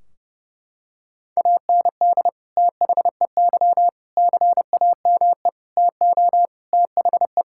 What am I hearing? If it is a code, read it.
Morse code, heard as a keyed tone, about 30 wpm, 700 Hz. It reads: AND THEY CAME TO THE
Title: Sentences from Top 1600 Words 30wpm